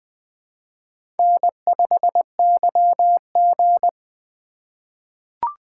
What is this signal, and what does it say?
Morse code, heard as a keyed tone, about 20 wpm, 700 Hz. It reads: N5YG
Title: USA Call Signs - 4 Characters - 3x Character Spacing - 20wpm